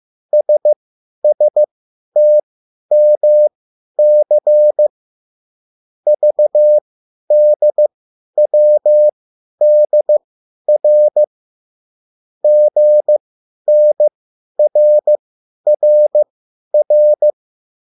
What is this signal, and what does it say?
Morse code, heard as a keyed tone, about 15 wpm, spaced as for 10 wpm, 600 Hz. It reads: SSTMC VDWDR GNRRR